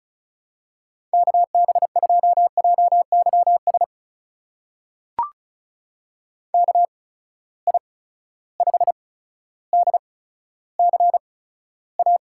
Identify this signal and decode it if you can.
Morse code, heard as a keyed tone, about 36 wpm, spaced as for 15 wpm, 700 Hz. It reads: KB2JYS KI5DCA
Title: USA Call Signs - 6 Characters - 8x Character Spacing - 35wpm